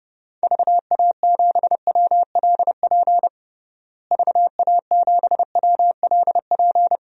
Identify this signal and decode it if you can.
Morse code, heard as a keyed tone, about 30 wpm, 700 Hz. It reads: VA7WLP VA7WLP
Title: DXCC-100 Call Signs - 6 Characters - Repeated 2x - 30wpm